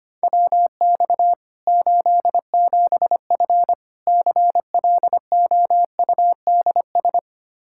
Morse code, heard as a keyed tone, about 25 wpm, 700 Hz. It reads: WX 87F CLOUDS
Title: K1USN SST QSO Practice 25wpm